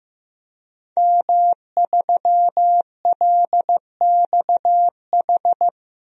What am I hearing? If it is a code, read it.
Morse code, heard as a keyed tone, about 15 wpm, 700 Hz. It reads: M3LXH
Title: DXCC-100 Call Signs - 5 Characters - Repeated 2x - 15wpm